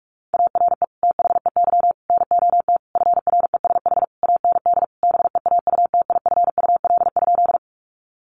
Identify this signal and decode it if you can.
Morse code, heard as a keyed tone, about 44 wpm, 700 Hz. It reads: ARE THEY NOT FRESH AND BEAUTIFUL?
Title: Sentences from Top 1200 Words 45wpm